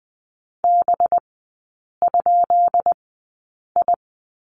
Morse code, heard as a keyed tone, about 20 wpm, spaced as for 8 wpm, 700 Hz. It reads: B?I
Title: 184 Lesson 30e - Letter b - Three Characters 20wpm